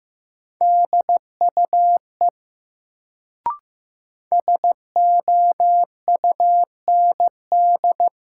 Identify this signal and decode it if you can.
Morse code, heard as a keyed tone, about 15 wpm, 700 Hz. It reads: DUE SOUND
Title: Sentences from Top 900 Words - New Word Drill - 15wpm